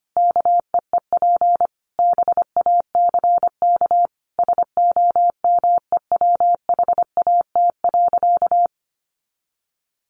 Morse code, heard as a keyed tone, about 25 wpm, 700 Hz. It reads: KEEP BACK SOMEWHAT.
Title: Sentences from Top 1300 Words 25wpm